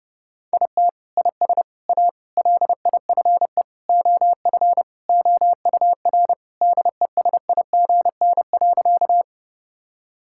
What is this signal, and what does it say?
Morse code, heard as a keyed tone, about 30 wpm, 700 Hz. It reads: IT IS A LIFE OF OUR DESIGN.
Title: Sentences from Top 1000 Words - Review Part 5 - 30wpm